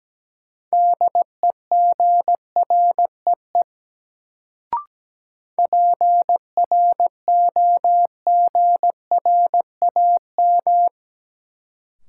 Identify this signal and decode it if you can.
Morse code, heard as a keyed tone, about 17 wpm, 700 Hz. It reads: DEGREE PROGRAM
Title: Sentences from Top 1500 Words - New Word Drill Alt - 17wpm